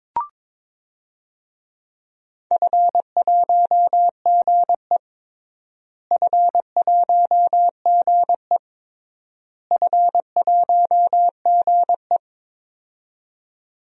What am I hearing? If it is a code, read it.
Morse code, heard as a keyed tone, about 23 wpm, 700 Hz. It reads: F1GE F1GE F1GE
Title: DXCC-100 Call Signs - 4 Characters - Repeated 3x - 22wpm